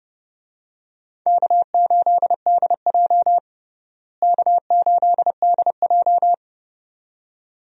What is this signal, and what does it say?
Morse code, heard as a keyed tone, about 32 wpm, 700 Hz. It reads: K8DJ K8DJ
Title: USA Call Signs - 4 Characters - Repeated 2x - 30wpm